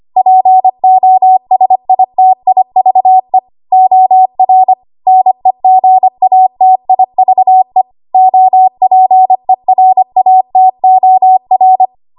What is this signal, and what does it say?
Morse code, heard as a keyed tone, about 25 wpm, 750 Hz. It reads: POSITIVE OR NEGATIVE OPERATOR
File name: Archive / 25 WPM / 25WPM